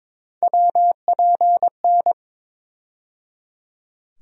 Morse code, heard as a keyed tone, about 22 wpm, 700 Hz. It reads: WPN
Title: USA Call Sign Postfixes - 3 Letters 22wpm